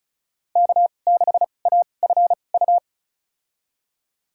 Morse code, heard as a keyed tone, about 36 wpm, spaced as for 25 wpm, 700 Hz. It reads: K6AFU
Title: USA Call Signs - 2x Character Spacing - 35wpm